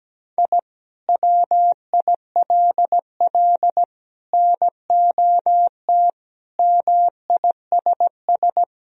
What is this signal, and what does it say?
Morse code, heard as a keyed tone, about 17 wpm, 700 Hz. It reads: I WILL NOT MISS
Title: Sentences from Top 1000 Words - Review Part 3 - 17wpm